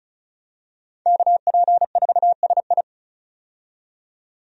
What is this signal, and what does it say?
Morse code, heard as a keyed tone, about 36 wpm, 700 Hz. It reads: KP4SI
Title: USA Call Signs - 5 Characters - 35wpm